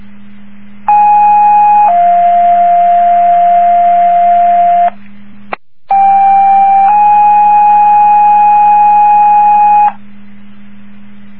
Two-tone page, heard as a one-second tone, then a three-second tone, about 830 then 700 Hz, about 790 then 830 Hz. It is heard from 0.9 to 9.9 s.